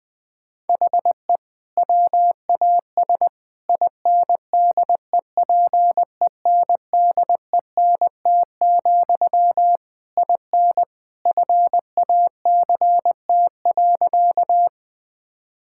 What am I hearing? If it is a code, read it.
Morse code, heard as a keyed tone, about 20 wpm, 700 Hz. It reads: HE WAS INDEPENDENT, IN FACT.
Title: Sentences from Top 1000 Words - Review Part 6 - 20wpm